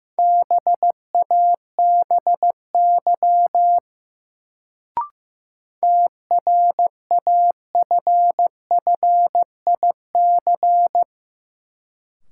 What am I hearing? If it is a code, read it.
Morse code, heard as a keyed tone, about 16 wpm, 700 Hz. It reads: BABY TRAFFIC
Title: Sentences from Top 1300 Words - New Word Drill - 15wpm